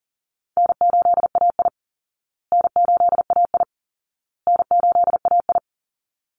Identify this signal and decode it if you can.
Morse code, heard as a keyed tone, about 40 wpm, 700 Hz. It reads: N8AI N8AI N8AI
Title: Call Signs Repeated 2x - 40wpm